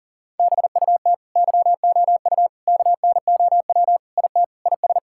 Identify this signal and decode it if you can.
Morse code, heard as a keyed tone, about 37 wpm, 700 Hz. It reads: BUT YOU KNOW IT IS
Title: Sentences from Top 1600 Words 40wpm